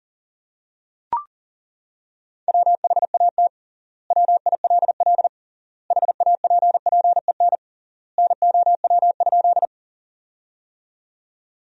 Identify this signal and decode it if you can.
Morse code, heard as a keyed tone, about 40 wpm, 700 Hz. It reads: WHAT WILL HAPPEN NOW?
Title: 3x Word-Spacing - Sentences from Top 400 Words - 40wpm